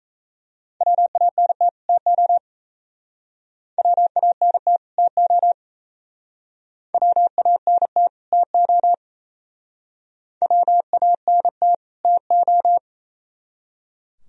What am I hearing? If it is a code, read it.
Morse code, heard as a keyed tone, about 36 wpm, 700 Hz. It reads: WANT TO WANT TO WANT TO WANT TO